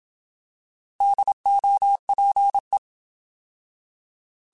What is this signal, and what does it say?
Morse code, heard as a keyed tone, about 27 wpm, 800 Hz. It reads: DOPE
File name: Alphabetical / Alphabetical mp3s / Ds @25